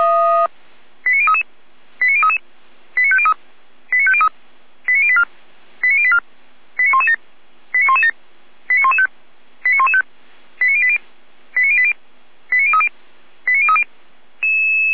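Five-tone selective call (ZVEI), heard as quick runs of five tones, tones of about 70 ms. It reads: 89033 89033 89503 89503 89084 89084 89117 89117 89115 89115 89080 89080 89030 89030
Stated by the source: ZVEI five-tone selective call received by SDR.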